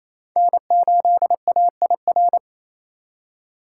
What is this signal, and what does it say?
Morse code, heard as a keyed tone, about 28 wpm, 700 Hz. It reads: N8AIR
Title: USA Call Signs - 5 Characters - 4x Character Spacing - 28wpm